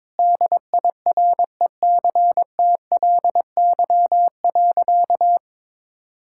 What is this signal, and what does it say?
Morse code, heard as a keyed tone, about 21 wpm, 700 Hz. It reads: DIRECTLY.
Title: Sentences from Top 1000 Words - Review Part 5 - 22wpm